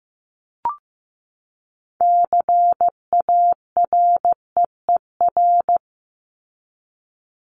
Morse code, heard as a keyed tone, about 15 wpm, 700 Hz. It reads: CAREER
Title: Top 751-1000 Words 15wpm